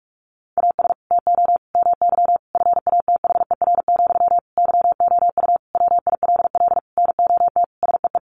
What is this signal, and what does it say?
Morse code, heard as a keyed tone, about 44 wpm, 700 Hz. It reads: AS TO MY FATHER, YOU WILL NOT SEE
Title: Sentences from Top 600 Words 45wpm